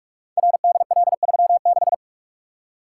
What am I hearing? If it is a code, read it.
Morse code, heard as a keyed tone, about 44 wpm, 700 Hz. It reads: RDL36
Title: Code Groups - 5 Letter:Numbers - 45wpm